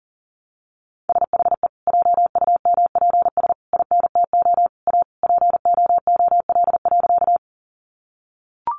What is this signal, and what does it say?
Morse code, heard as a keyed tone, about 40 wpm, 700 Hz. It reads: SHE JUMPS INTO A POOL.
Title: Easy Phrases 40wpm